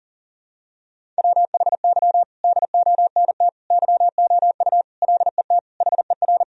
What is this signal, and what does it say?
Morse code, heard as a keyed tone, about 37 wpm, 700 Hz. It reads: WHY DONT YOU LET HER
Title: Sentences from Top 1000 Words - Review Part 1 - 40wpm